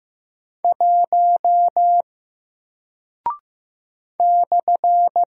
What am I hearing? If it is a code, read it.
Morse code, heard as a keyed tone, about 15 wpm, 700 Hz. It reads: 1 /
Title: Single Character (Letter:Number:Punctuation:Prosign) 15wpm